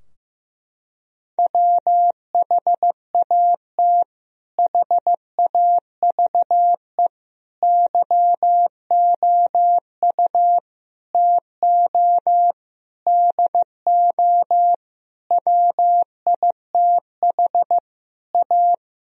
Morse code, heard as a keyed tone, about 15 wpm, 700 Hz. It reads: WHAT HAVE YOU TO DO WITH A
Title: Sentences from Top 1000 Words - Review Part 2 - 15wpm